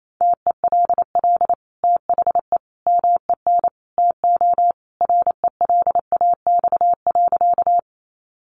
Morse code, heard as a keyed tone, about 28 wpm, 700 Hz. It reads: TELL THE MEN TO RELAX.
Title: Sentences from Top 1400 Words 28wpm